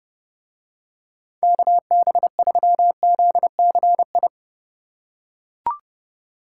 Morse code, heard as a keyed tone, about 28 wpm, 700 Hz. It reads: KB3ZCI